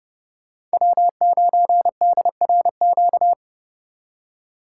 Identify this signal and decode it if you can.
Morse code, heard as a keyed tone, about 34 wpm, 700 Hz. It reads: W9DRQ